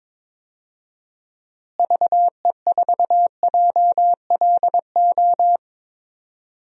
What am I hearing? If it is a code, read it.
Morse code, heard as a keyed tone, about 21 wpm, 700 Hz. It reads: VE4JLO
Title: DXCC-100 Call Signs - 6 Characters - Repeated 2x - 22wpm